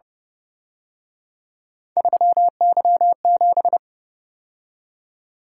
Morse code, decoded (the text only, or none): 3Y7